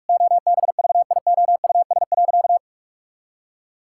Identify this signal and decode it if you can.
Morse code, heard as a keyed tone, about 46 wpm, 700 Hz. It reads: OBVIOUS.
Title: Sentences from Top 1600 Words 45wpm